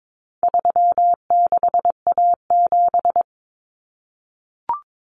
Morse code, heard as a keyed tone, about 22 wpm, 700 Hz. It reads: 36A7
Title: Common Cut Numbers - 4-Digit 22wpm